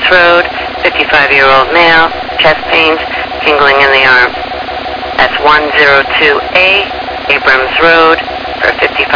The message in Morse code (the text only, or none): none